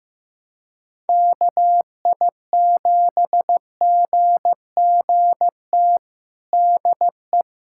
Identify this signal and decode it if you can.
Morse code, heard as a keyed tone, about 15 wpm, 700 Hz. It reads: KI7GGT DE
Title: USA Callsign DE Callsign - 6 Characters - 15wpm